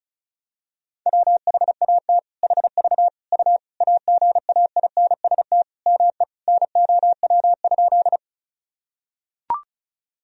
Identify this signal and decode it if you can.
Morse code, heard as a keyed tone, about 36 wpm, 700 Hz. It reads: WHAT HV U AGAINST ME NOW?
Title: Sentences from Top 400 Words and Abbreviations 35wpm